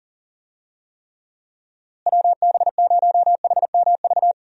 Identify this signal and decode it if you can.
Morse code, heard as a keyed tone, about 40 wpm, 700 Hz. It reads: WB0HMV
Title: USA Call Signs - 6 Characters - Repeated 3x - 40wpm